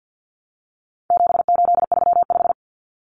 Morse code, heard as a keyed tone, about 50 wpm, 700 Hz. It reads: 7835
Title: Numbers - 4-Digit 50wpm